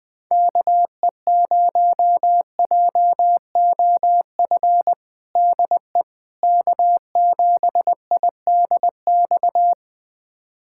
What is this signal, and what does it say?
Morse code, heard as a keyed tone, about 20 wpm, 700 Hz. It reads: KE0JOF DE K7IDX
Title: USA Callsign DE Callsign 20wpm